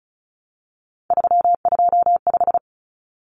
Morse code, heard as a keyed tone, about 34 wpm, 700 Hz. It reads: 325
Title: Numbers - 3-Digit 35wpm